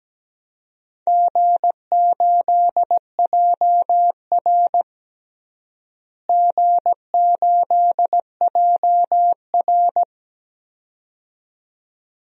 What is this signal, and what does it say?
Morse code, heard as a keyed tone, about 17 wpm, 700 Hz. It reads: G8JR G8JR
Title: DXCC-100 Call Signs - 4 Characters - Repeated 2x - 17wpm